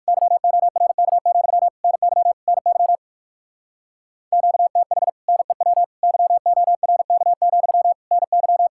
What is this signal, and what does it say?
Morse code, heard as a keyed tone, about 56 wpm, 700 Hz. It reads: YORK, NY NY QTH NEW YORK, NY
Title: Speed-Racing Common QSO Phrases - 45wpm